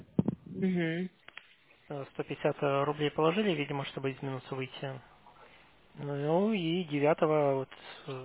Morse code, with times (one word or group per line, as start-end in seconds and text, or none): none